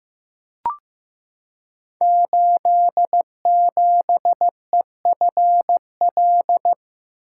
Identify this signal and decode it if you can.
Morse code, heard as a keyed tone, about 15 wpm, 700 Hz. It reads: 87EFL
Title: Code Groups - 5 Letter:Numbers - 15wpm